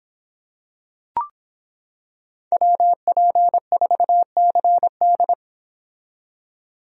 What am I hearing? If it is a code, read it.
Morse code, heard as a keyed tone, about 25 wpm, 700 Hz. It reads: WP4CD